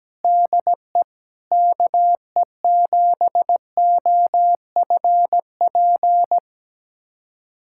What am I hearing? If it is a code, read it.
Morse code, heard as a keyed tone, about 17 wpm, 700 Hz. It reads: DE KE7OFP
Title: USA Callsign DE Callsign 17wpm